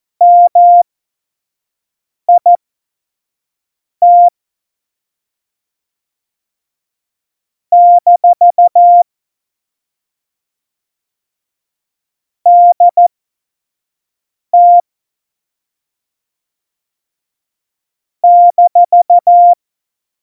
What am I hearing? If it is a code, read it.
Morse code, heard as a keyed tone, about 14 wpm, spaced as for 5 wpm, 700 Hz. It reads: MIT - DT -
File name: Morsekurs / Witze / Witz 30. April 2019, 25 F70